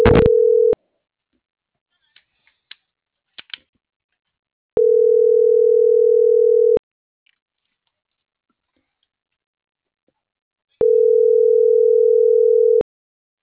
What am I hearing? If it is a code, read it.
Ringback tone (North America), 440 + 480 Hz.